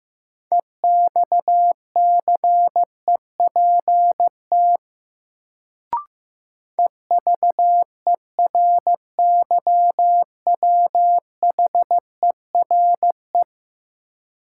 Morse code, heard as a keyed tone, about 15 wpm, 700 Hz. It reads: EXCEPT EVERYWHERE
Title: Sentences from Top 1600 Words - New Word Drill - 15wpm